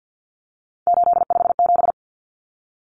Morse code, heard as a keyed tone, about 50 wpm, 700 Hz. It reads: 857
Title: Numbers - 3-Digit 50wpm